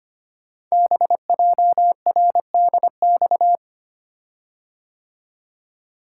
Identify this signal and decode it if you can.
Morse code, heard as a keyed tone, about 25 wpm, 700 Hz. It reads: BJRDX
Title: Code Groups - 5 Letters - 25wpm